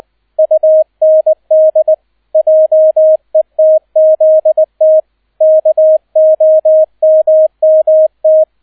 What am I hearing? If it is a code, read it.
Morse code, heard as a keyed tone, about 20 wpm, 630 Hz, with radio noise. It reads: UND JETZT KOMMT
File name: 002 Zweiter Streich 20wpm